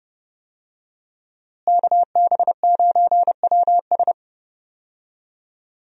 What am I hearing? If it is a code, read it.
Morse code, heard as a keyed tone, about 31 wpm, 700 Hz. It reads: KB9WS
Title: USA Call Signs - 5 Characters - Repeated 2x - 30wpm